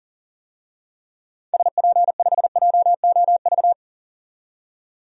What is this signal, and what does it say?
Morse code, heard as a keyed tone, about 37 wpm, 700 Hz. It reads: SP5JOV
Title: DXCC-100 Call Signs - 6 Characters - 40wpm